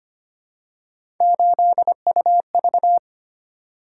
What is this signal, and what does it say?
Morse code, heard as a keyed tone, about 25 wpm, 700 Hz. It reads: 8UV
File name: Three Letter-Number Combinations - 25wpm